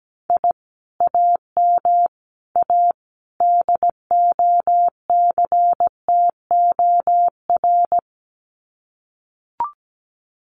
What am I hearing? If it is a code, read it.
Morse code, heard as a keyed tone, about 17 wpm, 700 Hz. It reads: I AM A DOCTOR
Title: Common QSO Phrases Encore 17wpm